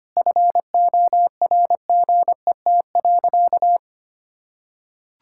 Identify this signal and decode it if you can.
Morse code, heard as a keyed tone, about 25 wpm, 700 Hz. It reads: FORGET.
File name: practice / 25wpm / Sentences with 4-Word Phrases - 25wpm